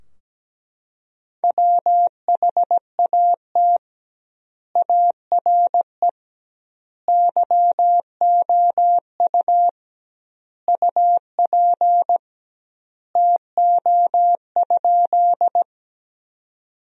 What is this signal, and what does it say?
Morse code, heard as a keyed tone, about 17 wpm, 700 Hz. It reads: WHAT ARE YOU UP TO?
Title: 2x Word-Spacing - Sentences from Top 100 Words - 17wpm